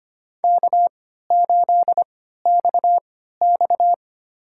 Morse code, heard as a keyed tone, about 25 wpm, spaced as for 14 wpm, 700 Hz. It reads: K8XX
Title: USA Call Signs - 3x Character Spacing - 25wpm